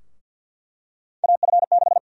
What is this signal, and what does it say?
Morse code, heard as a keyed tone, about 46 wpm, 700 Hz. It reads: AF6
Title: USA Call Sign Prefixes - Repeated 2x - 50wpm